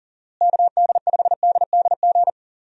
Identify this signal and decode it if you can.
Morse code, heard as a keyed tone, about 40 wpm, 700 Hz. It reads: KD5DDG
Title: USA Call Signs - 6 Characters - Repeated 2x - 40wpm